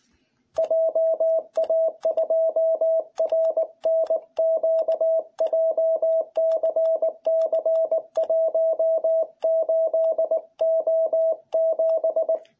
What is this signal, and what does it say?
Morse code, heard as a keyed tone, about 19 wpm, 660 Hz. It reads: JA2RNQ JCC18O7